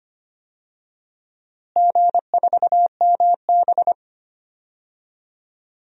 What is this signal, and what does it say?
Morse code, heard as a keyed tone, about 25 wpm, 700 Hz. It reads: G4MB
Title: DXCC-100 Call Signs - 4 Characters - Repeated 2x - 25wpm